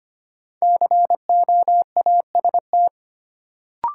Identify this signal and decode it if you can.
Morse code, heard as a keyed tone, about 25 wpm, 700 Hz. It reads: COAST
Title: Sentences from Top 1000 Words - Review Part 4 - New Word Drill - 25wpm